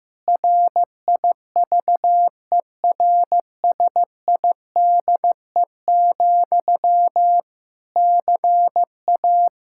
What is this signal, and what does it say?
Morse code, heard as a keyed tone, about 15 wpm, 700 Hz. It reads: RIVERSIDE, CA
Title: Practice QSO - Exchange 1 - Beginning - 15wpm